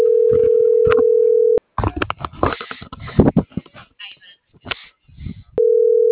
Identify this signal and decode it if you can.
Ringback tone (North America), 440 + 480 Hz.